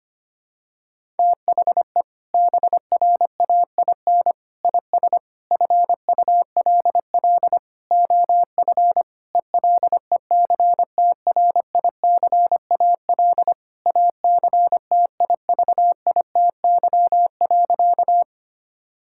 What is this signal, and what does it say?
Morse code, heard as a keyed tone, about 25 wpm, 700 Hz. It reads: THE BRAIN IS FULL OF ELECTRICAL ACTIVITY.
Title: Sentences from Top 1000 Words - Review Part 6 - 25wpm